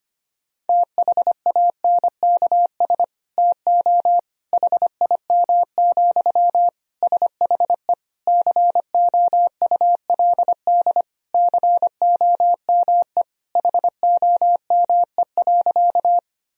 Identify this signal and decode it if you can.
Morse code, heard as a keyed tone, about 25 wpm, 700 Hz. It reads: THANKS TO HIM, SHE COULD COME HOME.
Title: Sentences from Top 1000 Words - Review Part 3 - 25wpm